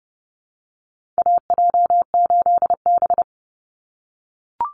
Morse code, heard as a keyed tone, about 30 wpm, 700 Hz. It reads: AJ8B
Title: Call Signs - 30wpm